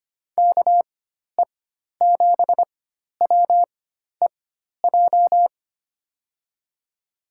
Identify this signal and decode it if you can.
Morse code, heard as a keyed tone, about 25 wpm, spaced as for 12 wpm, 700 Hz. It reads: KE7WEJ
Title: USA Call Signs - 6 Characters - 4x Character Spacing - 25wpm